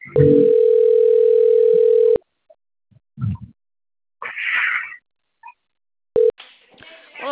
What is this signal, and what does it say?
Ringback tone (North America), 440 + 480 Hz, with speech.
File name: rg-1600-unknown- - - .12863